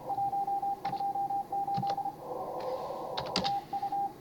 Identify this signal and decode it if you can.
Morse code, heard as a keyed tone, about 25 wpm, 800 Hz, with radio noise. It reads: 128 TU
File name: CW / HA5MA RHOB HANG CW 2017